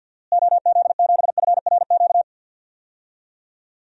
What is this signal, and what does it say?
Morse code, heard as a keyed tone, about 46 wpm, 700 Hz. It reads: OZ7FRQ
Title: DXCC-100 Call Signs - 6 Characters - Repeated 3x - 50wpm